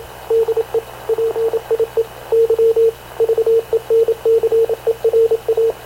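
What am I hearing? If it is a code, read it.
Morse code, heard as a keyed tone, about 27 wpm, 450 Hz, with radio noise. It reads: DE PIE Y VENCERA
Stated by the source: Havana Radio CLA press (PX) broadcast, recorded by Radio Officer Paul Dumesnil aboard mv CNAV QUEST/CZDO.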